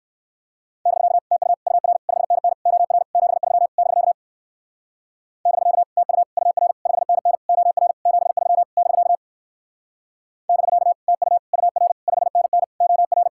Morse code, heard as a keyed tone, about 59 wpm, 700 Hz. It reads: <BK> TU UR 5NN OR 73 <BK> <BK> TU UR 5NN OR 73 <BK> <BK> TU UR 5NN OR